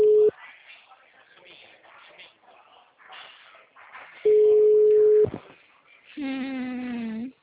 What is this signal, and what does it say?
Ringback tone (Europe, 425 Hz).
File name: out- -2004- - - .31460